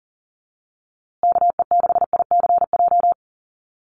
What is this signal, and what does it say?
Morse code, heard as a keyed tone, about 40 wpm, 700 Hz. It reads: KE6ICJ
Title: USA Call Signs Repeated 2x 40wpm